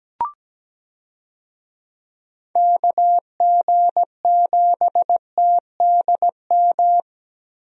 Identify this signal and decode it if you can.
Morse code, heard as a keyed tone, about 17 wpm, 700 Hz. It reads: KG7TDM
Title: USA Call Signs - 6 Characters - Repeated 2x - 17wpm